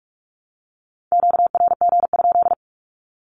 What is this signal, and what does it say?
Morse code, heard as a keyed tone, about 44 wpm, 700 Hz. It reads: QRG?
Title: Q-Codes 45wpm